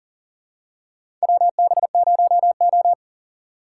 Morse code, heard as a keyed tone, about 40 wpm, 700 Hz. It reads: WB0O
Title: USA Call Signs - 4 Characters - 6x Character Spacing - 40wpm